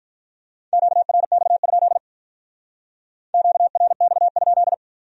Morse code, heard as a keyed tone, about 44 wpm, 700 Hz. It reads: QRX? QRX?